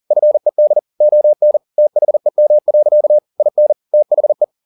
Morse code, heard as a keyed tone, about 40 wpm, 600 Hz. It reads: FED ON THEM. IN THE